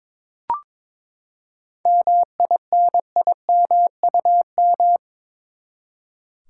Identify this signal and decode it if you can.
Morse code, heard as a keyed tone, about 23 wpm, 700 Hz. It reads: MINIMUM
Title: Seven Letter Words - 22wpm